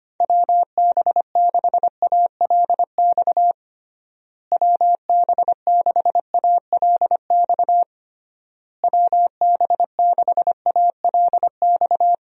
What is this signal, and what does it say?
Morse code, heard as a keyed tone, about 25 wpm, 700 Hz. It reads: WB6ALX WB6ALX WB6ALX
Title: USA Call Signs - 6 Characters - Repeated 3x - 25wpm